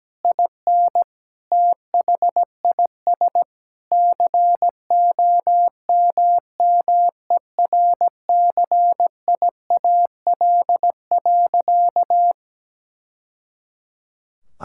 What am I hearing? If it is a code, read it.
Morse code, heard as a keyed tone, about 17 wpm, 700 Hz. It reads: IN THIS COMMERCIAL.
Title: Sentences from Top 1000 Words - Review Part 5 - 17wpm